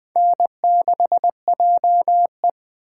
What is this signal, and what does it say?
Morse code, heard as a keyed tone, about 20 wpm, 700 Hz. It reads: N6JE
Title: USA Call Signs - 4 Characters - 8x Character Spacing - 20wpm